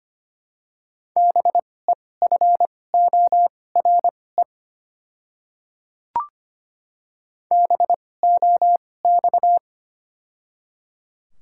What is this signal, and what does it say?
Morse code, heard as a keyed tone, about 25 wpm, spaced as for 18 wpm, 700 Hz. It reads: BEFORE BOX
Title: 2x Character Spacing - Top 500 Words - 25wpm